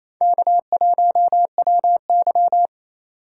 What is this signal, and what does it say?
Morse code, heard as a keyed tone, about 35 wpm, 700 Hz. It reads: K1WY
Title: Speed Racing - USA 1x2 Call Signs 28wpm